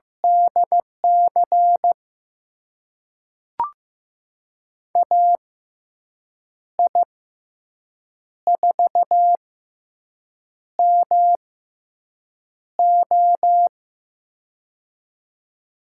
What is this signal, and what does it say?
Morse code, heard as a keyed tone, about 15 wpm, spaced as for 6 wpm, 700 Hz. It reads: DC AI4MO
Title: USA Call Signs - 5 Characters - 6x Character Spacing - 15wpm